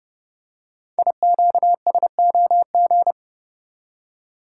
Morse code, heard as a keyed tone, about 29 wpm, 690 Hz. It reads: IQSOG
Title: Code Groups - 5 Letters - 30wpm